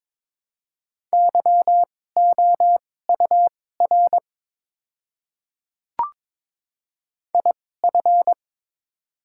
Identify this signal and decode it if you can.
Morse code, heard as a keyed tone, about 22 wpm, spaced as for 16 wpm, 700 Hz. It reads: YOUR IF
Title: 2x Character Spacing - Top 100 Words - 22wpm